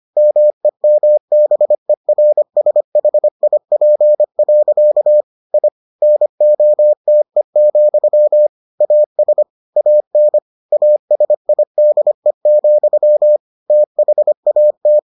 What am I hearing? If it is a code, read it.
Morse code, heard as a keyed tone, about 25 wpm, 600 Hz. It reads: MEMBERSHIP. I NOTE, AS AN ASIDE, THAT